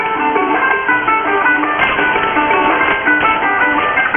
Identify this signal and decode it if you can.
Emergency Alert System (SAME) tones, heard as harsh buzzing bursts. It is heard from 0.6 to 3.2 s.
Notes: Shortwave Pirate Radio Station, Radio Cinco De Mayo. Recorded starting around 00:17 UTC on 05 May 2013 on 6925 kHz USB.